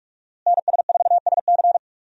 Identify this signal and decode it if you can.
Morse code, heard as a keyed tone, about 41 wpm, 700 Hz. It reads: NS4SC
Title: USA Call Signs - 5 Characters - Repeated 2x - 45wpm